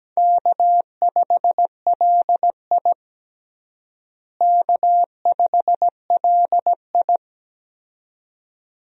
Morse code, heard as a keyed tone, about 17 wpm, 700 Hz. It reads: K5LI K5LI
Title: USA Call Signs - 4 Characters - Repeated 2x - 17wpm